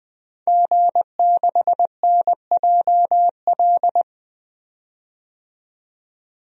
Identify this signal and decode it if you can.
Morse code, heard as a keyed tone, about 20 wpm, 700 Hz. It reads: G6NJL
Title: DXCC-100 Call Signs - 5 Characters - 20wpm